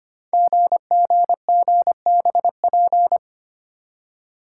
Morse code, heard as a keyed tone, about 25 wpm, 700 Hz. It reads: GGGBP
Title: Code Groups - 5 Letters - 25wpm